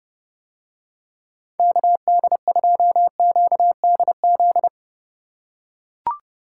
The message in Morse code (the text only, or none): KD2QDZ